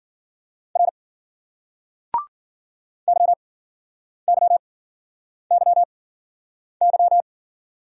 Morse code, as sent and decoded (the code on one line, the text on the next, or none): .-. / -.-- -.-- -.-- -.--
R YYYY